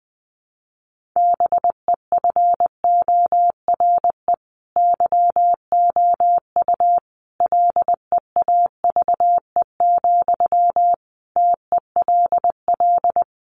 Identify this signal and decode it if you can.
Morse code, heard as a keyed tone, about 20 wpm, 700 Hz. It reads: BEFORE YOU LEAVE, TELL
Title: Sentences from Top 400 Words 20wpm